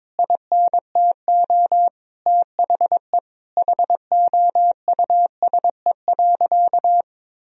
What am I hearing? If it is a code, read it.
Morse code, heard as a keyed tone, about 22 wpm, 700 Hz. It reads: INTO THE HOUSE.
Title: Sentences from Top 1000 Words - Review Part 1 - 22wpm